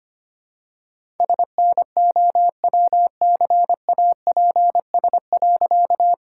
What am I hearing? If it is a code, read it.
Morse code, heard as a keyed tone, about 25 wpm, 700 Hz. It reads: SNOWCAPS.
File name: practice / 25wpm / All-Age Appropriate Jokes - 25wpm